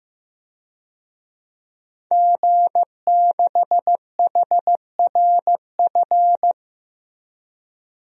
Morse code, heard as a keyed tone, about 15 wpm, 700 Hz. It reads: G6HRF
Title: DXCC-100 Call Signs - 5 Characters - Repeated 2x - 15wpm